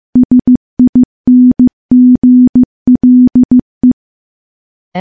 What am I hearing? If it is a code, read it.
Morse code, heard as a keyed tone, about 15 wpm, 250 Hz. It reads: SINGLE
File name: morse / morse-training-15WPM